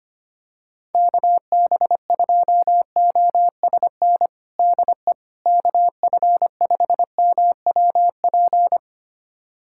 Morse code, heard as a keyed tone, about 25 wpm, 700 Hz. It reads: KB2OSN DE KF5MWP
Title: USA Callsign DE Callsign 25wpm